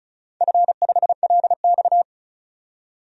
Morse code, heard as a keyed tone, about 34 wpm, 700 Hz. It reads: F5LX
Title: DXCC-100 Call Signs - 4 Characters - 35wpm